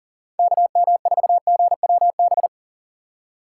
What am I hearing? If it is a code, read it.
Morse code, heard as a keyed tone, about 40 wpm, 700 Hz. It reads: KM4GWB
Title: USA Call Signs - 6 Characters - 6x Character Spacing - 40wpm